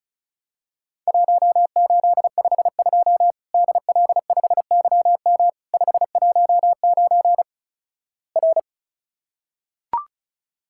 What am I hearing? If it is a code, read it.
Morse code, heard as a keyed tone, about 36 wpm, 700 Hz. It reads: 1852 DL5YM 519 R
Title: Worked All Europe DX Contest - QTC Exchange - 35wpm